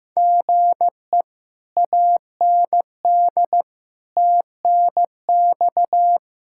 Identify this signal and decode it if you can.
Morse code, heard as a keyed tone, about 16 wpm, 700 Hz. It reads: GE AND TNX
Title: Speed-Racing Common QSO Phrases Encore - 15wpm